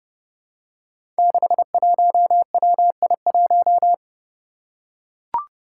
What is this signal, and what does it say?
Morse code, heard as a keyed tone, about 30 wpm, 700 Hz. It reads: 61WI1
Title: Five Challenging Letter-Number Combinations 30wpm